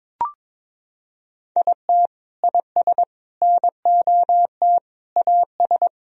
Morse code, heard as a keyed tone, about 22 wpm, 700 Hz. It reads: IT IS NOT AS
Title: Sentences from Top 1000 Words - Review Part 1 - 22wpm